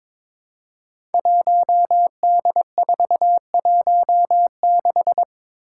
Morse code, heard as a keyed tone, about 22 wpm, 700 Hz. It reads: 1D416
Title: Five Challenging Letter-Number Combinations 22wpm